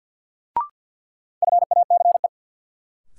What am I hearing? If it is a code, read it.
Morse code, heard as a keyed tone, about 53 wpm, 700 Hz. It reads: FACE
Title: Sentences from Top 300 Words - New Word Drill - 50wpm